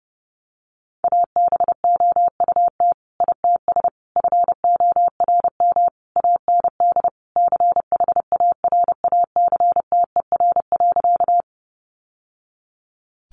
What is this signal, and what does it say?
Morse code, heard as a keyed tone, about 30 wpm, 700 Hz. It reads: ABOUT ITS FORM AND CHARACTER.
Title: Sentences from Top 1200 Words 30wpm